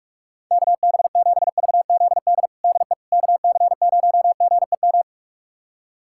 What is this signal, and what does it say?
Morse code, heard as a keyed tone, about 45 wpm, 700 Hz. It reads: KB7VZD DE KC0GEM
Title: USA Callsign DE Callsign 45wpm